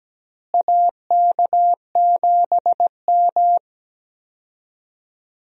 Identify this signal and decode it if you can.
Morse code, heard as a keyed tone, about 17 wpm, 700 Hz. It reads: AK7M
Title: USA Call Signs - 4 Characters - Repeated 2x - 17wpm